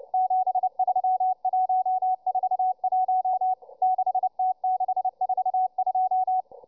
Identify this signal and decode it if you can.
Morse code, heard as a keyed tone, about 29 wpm, 740 Hz, with radio noise. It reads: 73141 6T642